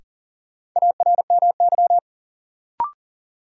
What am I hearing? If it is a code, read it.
Morse code, heard as a keyed tone, about 40 wpm, 700 Hz. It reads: ARMY
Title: Sentences from Top 800 Words - New Word Drill - 40wpm